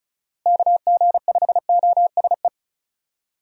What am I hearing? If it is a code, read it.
Morse code, heard as a keyed tone, about 34 wpm, 700 Hz. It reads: KG5OSE
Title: USA Call Signs - 6 Characters - Repeated 2x - 35wpm